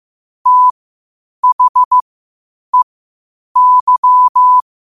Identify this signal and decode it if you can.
Morse code, heard as a keyed tone, about 15 wpm, spaced as for 8 wpm, 1000 Hz. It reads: THEY
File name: qotdfiles / 8wpm